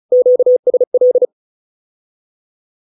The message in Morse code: QSL

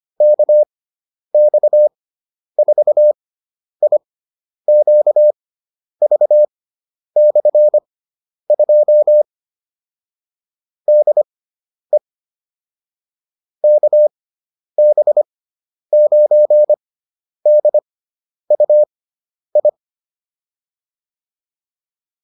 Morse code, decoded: KX4IQV/2 DE KB9DUI

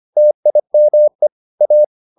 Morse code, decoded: TIME A

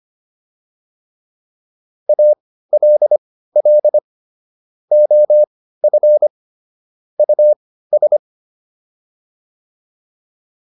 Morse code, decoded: ALL OF US

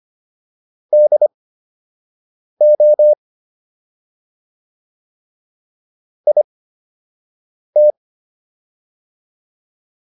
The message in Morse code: DO IT

